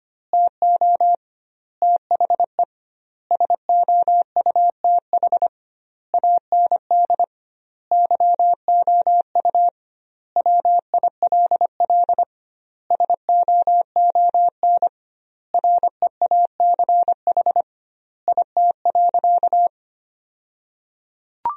Morse code, heard as a keyed tone, about 25 wpm, 700 Hz. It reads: TO THE SOUTH AND YOU WILL SOON REACH IT.
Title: 2x Word-Spacing - Sentences from Top 400 Words - 25wpm